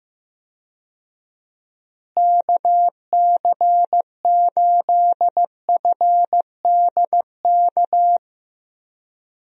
Morse code, heard as a keyed tone, about 15 wpm, 700 Hz. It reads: KC8FDK
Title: USA Call Signs - 6 Characters - Repeated 3x - 15wpm